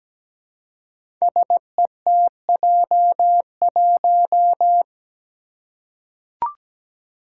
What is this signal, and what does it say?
Morse code, heard as a keyed tone, about 17 wpm, 700 Hz. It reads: SETJ1
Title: Five Challenging Letter-Number Combinations 17wpm